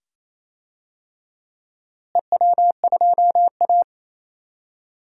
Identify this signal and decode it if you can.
Morse code, heard as a keyed tone, about 28 wpm, 700 Hz. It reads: EW2A
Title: DXCC-100 Call Signs - 4 Characters - Repeated 2x - 28wpm